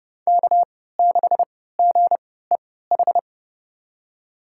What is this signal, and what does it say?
Morse code, heard as a keyed tone, about 32 wpm, spaced as for 17 wpm, 700 Hz. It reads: K6GEH
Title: USA Call Signs - 5 Characters - 3x Character Spacing - 30wpm